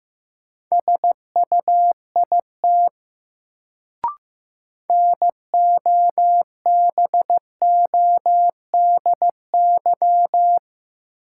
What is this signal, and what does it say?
Morse code, heard as a keyed tone, about 15 wpm, 700 Hz. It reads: SUIT NOBODY
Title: Sentences from Top 1600 Words - New Word Drill Alt - 15wpm